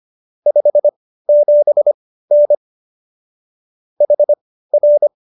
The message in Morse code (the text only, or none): 57N HR